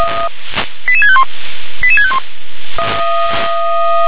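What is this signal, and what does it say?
Five-tone selective call (ZVEI), heard as quick runs of five tones, tones of about 70 ms. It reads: 88651 88651
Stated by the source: ZVEI five-tone call 88797 received by SDR.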